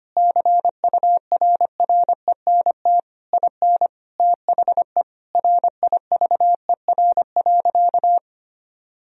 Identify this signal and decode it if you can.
Morse code, heard as a keyed tone, about 25 wpm, 700 Hz. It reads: CURRENT IN THE RIVER.
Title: Sentences from Top 1000 Words - Review Part 4 - 25wpm